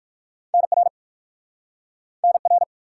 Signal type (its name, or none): Morse code